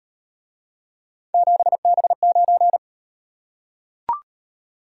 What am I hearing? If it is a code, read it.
Morse code, heard as a keyed tone, about 33 wpm, 700 Hz. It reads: 7B9